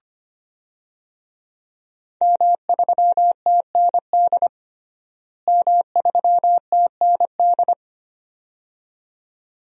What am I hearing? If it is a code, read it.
Morse code, heard as a keyed tone, about 25 wpm, 700 Hz. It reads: M3TND M3TND
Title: DXCC-100 Call Signs - 5 Characters - Repeated 2x - 25wpm